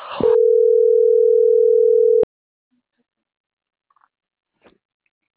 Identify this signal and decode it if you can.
Ringback tone (North America), 440 + 480 Hz.